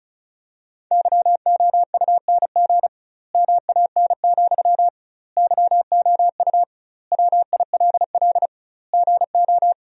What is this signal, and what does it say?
Morse code, heard as a keyed tone, about 36 wpm, 700 Hz. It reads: YOUNG MAN, YOU WILL GO
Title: 2x Word-Spacing - Sentences from Top 400 Words - 35wpm